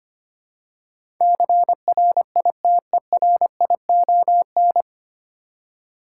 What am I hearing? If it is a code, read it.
Morse code, heard as a keyed tone, about 25 wpm, 700 Hz. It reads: CRITERION
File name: practice / 25wpm / 9 Letter Words - First 3 Letters Given - 25wpm